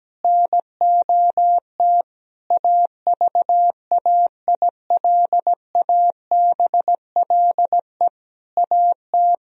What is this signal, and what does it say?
Morse code, heard as a keyed tone, about 17 wpm, 700 Hz. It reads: NOT AVAILABLE AT
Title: Sentences from Top 1000 Words - Review Part 3 - 17wpm